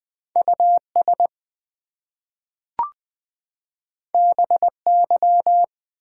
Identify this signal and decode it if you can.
Morse code, heard as a keyed tone, about 20 wpm, 700 Hz. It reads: US BY
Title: Two Letter Words - 20wpm